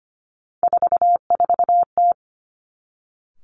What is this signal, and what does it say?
Morse code, heard as a keyed tone, about 25 wpm, 700 Hz. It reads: 44T
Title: Three Challenging Letter-Number Combinations 25wpm